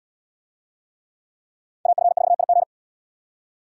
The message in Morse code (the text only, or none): AF4IG